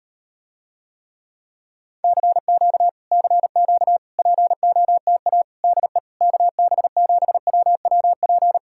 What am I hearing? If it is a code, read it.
Morse code, heard as a keyed tone, about 38 wpm, 700 Hz. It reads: CQ CQ POTA DE KB7WWP